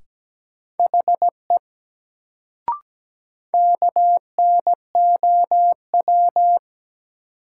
Morse code, heard as a keyed tone, about 17 wpm, 700 Hz. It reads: HE KNOW
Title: Sentences from Top 100 Words - New Word Drill - 17wpm